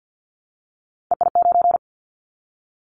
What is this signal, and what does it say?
Morse code, heard as a keyed tone, about 50 wpm, 700 Hz. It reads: EI9
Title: Callsign Prefixes for Top 50 DXCC - 50wpm